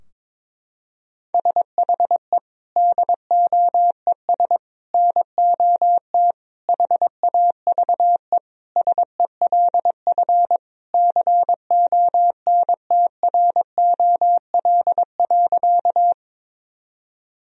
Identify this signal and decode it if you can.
Morse code, heard as a keyed tone, about 22 wpm, 700 Hz. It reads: SHE DOES NOT HAVE SELF CONTROL.
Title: Sentences from Top 1000 Words - Review Part 3 - 22wpm